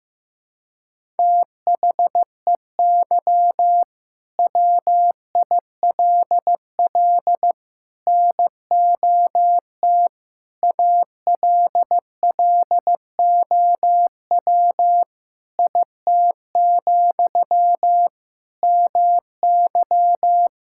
Morse code, heard as a keyed tone, about 16 wpm, 700 Hz. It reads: THEY WILL NOT ALLOW IT, MY
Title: Sentences from Top 1000 Words - Review Part 5 - 15wpm